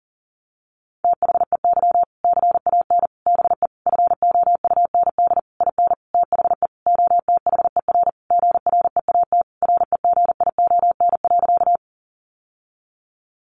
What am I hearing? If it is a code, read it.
Morse code, heard as a keyed tone, about 40 wpm, 700 Hz. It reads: THEY CAN BE FOUND IN THE OTHER GREAT REGION.
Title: Sentences from Top 1500 Words 40wpm